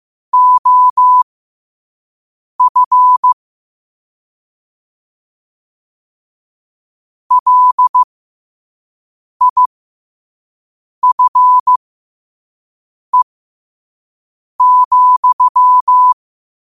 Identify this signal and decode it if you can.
Morse code, heard as a keyed tone, about 15 wpm, spaced as for 5 wpm, 1000 Hz. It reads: OF LIFE,
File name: qotdfiles / 5wpm